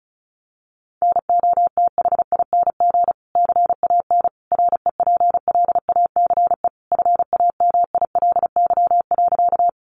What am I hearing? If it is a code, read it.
Morse code, heard as a keyed tone, about 34 wpm, 700 Hz. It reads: NOTHING CAN REPLACE FAMILY.
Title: Sentences from Top 1400 Words 35wpm